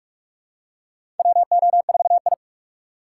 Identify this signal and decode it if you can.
Morse code, heard as a keyed tone, about 46 wpm, 700 Hz. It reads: WO4I
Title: USA Call Signs - 2x Character Spacing - 45wpm